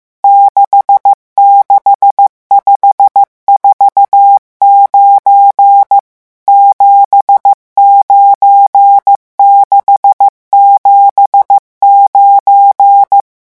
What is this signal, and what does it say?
Morse code, heard as a keyed tone, about 15 wpm, 800 Hz. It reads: 66549 79679
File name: morse mp3 / revision11 chiffres vitesse 14